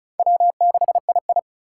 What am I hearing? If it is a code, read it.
Morse code, heard as a keyed tone, about 36 wpm, 700 Hz. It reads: W6II
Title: USA Call Signs - 4 Characters - 3x Character Spacing - 35wpm